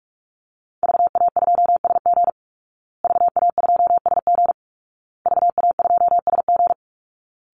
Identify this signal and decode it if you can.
Morse code, heard as a keyed tone, about 44 wpm, 700 Hz. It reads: VA2SG VA2SG VA2SG
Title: Call Signs Repeated 2x - 45wpm